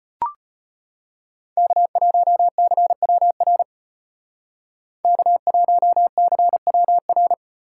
Morse code, heard as a keyed tone, about 34 wpm, 700 Hz. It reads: K1CWR K1CWR